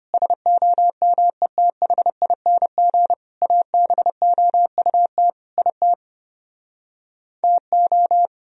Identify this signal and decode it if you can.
Morse code, heard as a keyed tone, about 33 wpm, 700 Hz. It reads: SOMETHING ABOUT IT TO